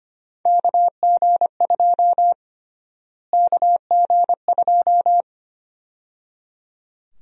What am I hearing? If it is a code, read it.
Morse code, heard as a keyed tone, about 25 wpm, 700 Hz. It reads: KG2 KG2
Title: USA Call Sign Prefixes - Repeated 2x - 25wpm